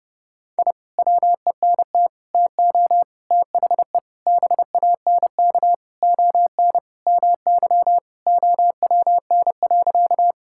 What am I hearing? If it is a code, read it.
Morse code, heard as a keyed tone, about 30 wpm, 700 Hz. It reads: I WENT TO THE BANK ON MY OWN.
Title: Sentences from Top 1000 Words - Review Part 6 - 30wpm